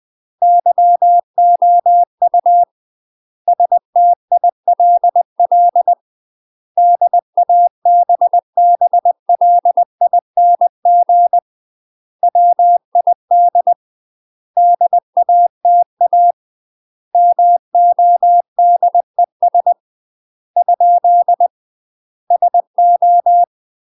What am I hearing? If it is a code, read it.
Morse code, heard as a keyed tone, about 20 wpm, 700 Hz. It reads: YOU STILL DABBLING WID DATA MODES ? SO
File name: UK MASTER QSOs TRAINING / 20WPM / UK MASTER QSOs 0030